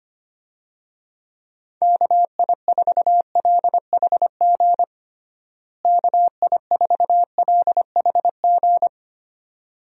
Morse code, heard as a keyed tone, about 25 wpm, 700 Hz. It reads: KI4LHG KI4LHG
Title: USA Call Signs - 6 Characters - Repeated 2x - 25wpm